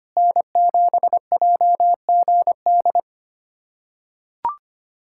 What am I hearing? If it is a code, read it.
Morse code, heard as a keyed tone, about 25 wpm, 700 Hz. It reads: N7JGD
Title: USA Callsign DE Callsign - 5 Characters - 25wpm